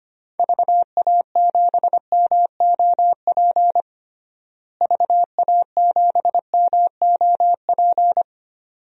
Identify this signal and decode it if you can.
Morse code, heard as a keyed tone, about 25 wpm, 700 Hz. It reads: VA7MOP VA7MOP
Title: DXCC-100 Call Signs - 6 Characters - Repeated 3x - 25wpm